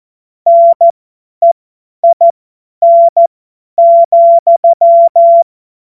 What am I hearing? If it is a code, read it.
Morse code, heard as a keyed tone, about 14 wpm, spaced as for 10 wpm, 680 Hz. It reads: NEIN,
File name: Morsekurs / Witze / Witz 28. September 2021, 50 F70